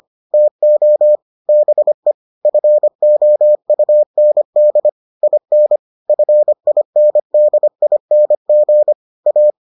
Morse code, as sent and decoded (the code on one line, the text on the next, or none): - --- / -... / ..-. --- ..- -. -.. / .. -. / ..-. .. -. -.. .. -. --. / .-
TO BE FOUND IN FINDING A